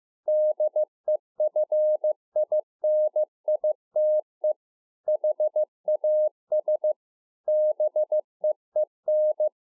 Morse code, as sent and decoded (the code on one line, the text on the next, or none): -.. ..-. .. -. .. - . / .... .- ... / -... . -.
DEFINITE HAS BEEN